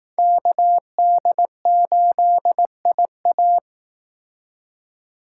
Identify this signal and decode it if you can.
Morse code, heard as a keyed tone, about 19 wpm, 700 Hz. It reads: KD8IA